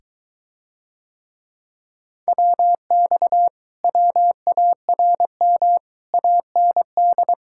Morse code, heard as a keyed tone, about 23 wpm, 700 Hz. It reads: WX WARM AND